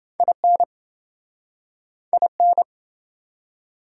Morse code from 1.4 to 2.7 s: IN

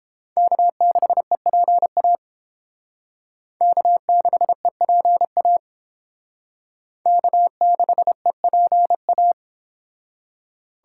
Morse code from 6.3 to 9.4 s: K6EPA

